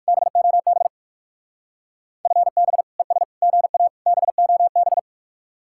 Morse code, heard as a keyed tone, about 54 wpm, 700 Hz. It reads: BOB FB ES GA BOB
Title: Speed-Racing Common QSO Phrases - 45wpm